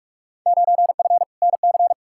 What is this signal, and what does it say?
Morse code, heard as a keyed tone, about 46 wpm, 700 Hz. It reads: 9F NC
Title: Field Day Exchanges Encore - 45wpm